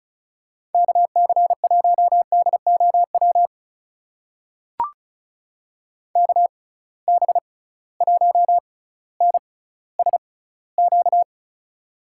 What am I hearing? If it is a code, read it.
Morse code, heard as a keyed tone, about 36 wpm, spaced as for 18 wpm, 700 Hz. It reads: KC1DOW KB1NSQ
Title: USA Call Signs - 6 Characters - 6x Character Spacing - 35wpm